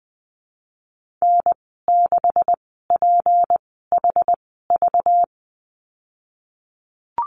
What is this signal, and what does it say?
Morse code, heard as a keyed tone, about 20 wpm, spaced as for 15 wpm, 700 Hz. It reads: N6PHV
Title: 207 Lesson 33g - Letter j - Call Signs 20wpm